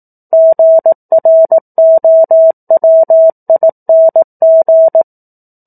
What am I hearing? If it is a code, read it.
Morse code, heard as a keyed tone, about 18 wpm, 650 Hz. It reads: GROWING
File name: 04-Chapter-04 18 WPM